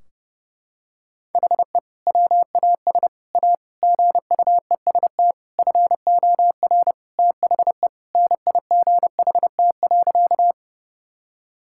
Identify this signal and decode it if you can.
Morse code, heard as a keyed tone, about 30 wpm, 700 Hz. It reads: HE WAS A GUEST FOR THE NIGHT.
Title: Sentences from Top 1600 Words 30wpm